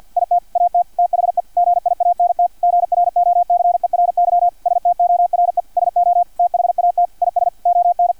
Morse code, heard as a keyed tone, about 50 wpm, 700 Hz, with radio noise. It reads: IT AT THE GIANT GROCERY STORE SO THAT IS ON
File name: archived / CRSnet